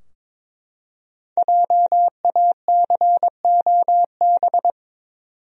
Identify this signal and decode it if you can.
Morse code, heard as a keyed tone, about 21 wpm, 700 Hz. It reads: JACOB
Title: USA Common Names - 22wpm